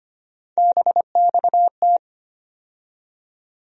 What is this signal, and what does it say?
Morse code, heard as a keyed tone, about 25 wpm, 700 Hz. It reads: BXT
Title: USA Call Sign Postfixes - 3 Letters 25wpm